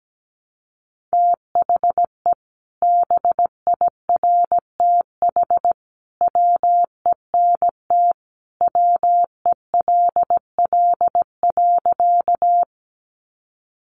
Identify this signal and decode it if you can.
Morse code, heard as a keyed tone, about 17 wpm, 700 Hz. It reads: THE BIRTH WENT WELL.
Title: Sentences from Top 800 Words 17wpm